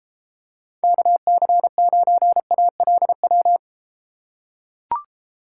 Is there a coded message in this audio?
Yes: Morse code from 0.8 to 3.6 s, heard as a keyed tone, about 33 wpm, 700 Hz. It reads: KC9ALW